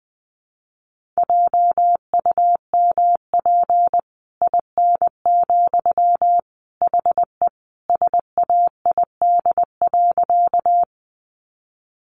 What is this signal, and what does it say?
Morse code, heard as a keyed tone, about 20 wpm, 700 Hz. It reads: JUMP IN, HE SAID.
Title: Sentences from Top 1300 Words 20wpm